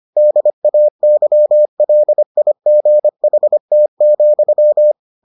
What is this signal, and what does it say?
Morse code, heard as a keyed tone, about 25 wpm, 600 Hz. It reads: DAYLIGHT,